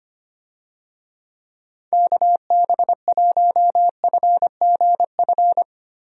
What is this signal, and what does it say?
Morse code, heard as a keyed tone, about 25 wpm, 700 Hz. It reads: KB1FGF
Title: USA Call Signs - 6 Characters - Repeated 2x - 25wpm